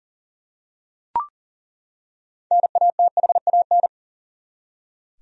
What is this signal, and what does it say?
Morse code, heard as a keyed tone, about 40 wpm, 700 Hz. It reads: NATHAN
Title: USA Common Names - 40wpm